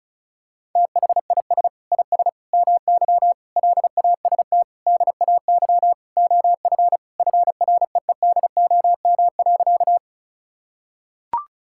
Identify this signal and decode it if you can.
Morse code, heard as a keyed tone, about 36 wpm, 700 Hz. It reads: THIS IS MY LAST DAY OF FREEDOM.
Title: Sentences from Top 1000 Words - Review Part 5 - 35wpm